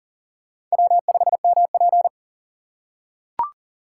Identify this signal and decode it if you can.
Morse code, heard as a keyed tone, about 40 wpm, 700 Hz. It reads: W5MP
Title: USA Call Signs - 4 Characters - 2x Character Spacing - 40wpm